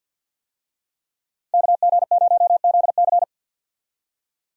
Morse code, heard as a keyed tone, about 50 wpm, 700 Hz. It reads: KG0ZC
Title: USA Call Signs - 5 Characters - 50wpm